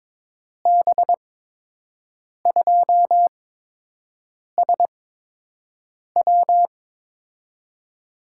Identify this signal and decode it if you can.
Morse code, heard as a keyed tone, about 22 wpm, spaced as for 6 wpm, 700 Hz. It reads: B2SW
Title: USA Call Signs - 5 Characters - 8x Character Spacing - 22wpm